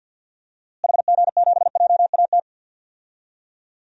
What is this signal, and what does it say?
Morse code, heard as a keyed tone, about 52 wpm, 700 Hz. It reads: HG7JAT
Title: DXCC-100 Call Signs - 6 Characters - Repeated 2x - 50wpm